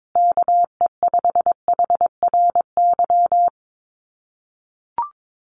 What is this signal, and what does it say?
Morse code, heard as a keyed tone, about 22 wpm, 700 Hz. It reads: KE5HRY
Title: USA Call Signs Repeated 2x 22wpm